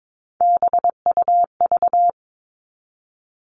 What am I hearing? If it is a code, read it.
Morse code, heard as a keyed tone, about 22 wpm, 700 Hz. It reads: BUV
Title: All Cut Numbers - 3-Digit 22wpm